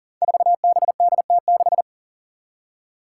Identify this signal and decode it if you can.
Morse code, heard as a keyed tone, about 40 wpm, 700 Hz. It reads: 4BDT6
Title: Five Challenging Letter-Number Combinations 40wpm